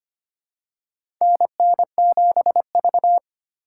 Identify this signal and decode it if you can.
Morse code, heard as a keyed tone, about 25 wpm, 700 Hz. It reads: NN7V
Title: USA Call Signs - 4x Character Spacing - 25wpm